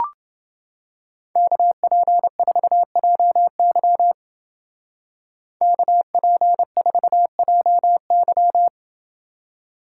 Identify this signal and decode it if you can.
Morse code, heard as a keyed tone, about 29 wpm, 700 Hz. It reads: KP4JY KP4JY